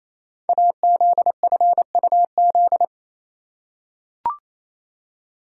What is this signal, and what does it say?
Morse code, heard as a keyed tone, about 28 wpm, 700 Hz. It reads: AZFUZ
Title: Code Groups - 5 Letters - 28wpm